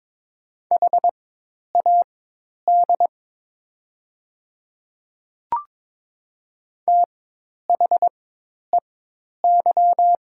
Morse code, heard as a keyed tone, about 22 wpm, spaced as for 10 wpm, 700 Hz. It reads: HAD THEY
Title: 4x Character Spacing - Top 100 Words - 22wpm